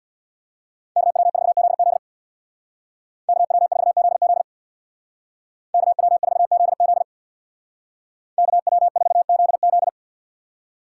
Morse code, heard as a keyed tone, about 60 wpm, 700 Hz. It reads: KW4ZZ KW4ZZ KW4ZZ KW4ZZ
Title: Speed Racing - USA 2x2 Call Signs 50wpm